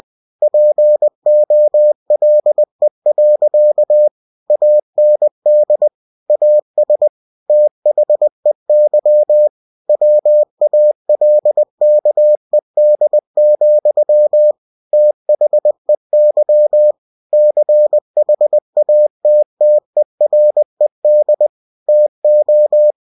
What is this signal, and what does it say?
Morse code, heard as a keyed tone, about 20 wpm, 600 Hz. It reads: POLE. AND AS THEY WALKED, THEY CHATTERED TO
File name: winnie-20wpm